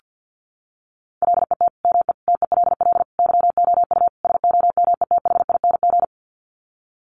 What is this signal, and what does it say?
Morse code, heard as a keyed tone, about 50 wpm, 700 Hz. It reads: LET ME TELL YOU SOMETHING
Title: Sets of 5 Words 50wpm